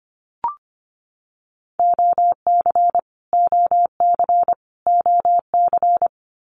Morse code, heard as a keyed tone, about 25 wpm, 700 Hz. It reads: OC OC OC
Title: Abbreviations Repeated 3x - 25wpm